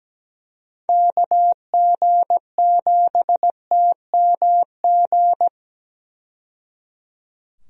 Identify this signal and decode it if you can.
Morse code, heard as a keyed tone, about 17 wpm, 700 Hz. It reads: KG7TMG
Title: USA Call Signs - 6 Characters - 17wpm